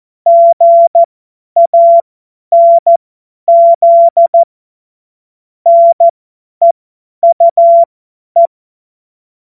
Morse code, heard as a keyed tone, about 14 wpm, spaced as for 10 wpm, 680 Hz. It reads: GANZ NEUE
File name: Morsekurs / Witze / Witz 19 März 2024 50 F70